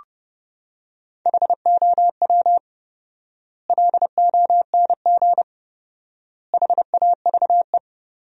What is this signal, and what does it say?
Morse code, heard as a keyed tone, about 30 wpm, 700 Hz. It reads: HOW LONG HAVE
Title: 4x Word-Spacing - Sentences from Top 200 Words - 30wpm